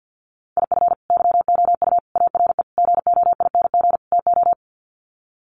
Morse code, heard as a keyed tone, about 50 wpm, 700 Hz. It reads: IF YOU ARE GOING TO
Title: Sets of 5 Words Encore2 50wpm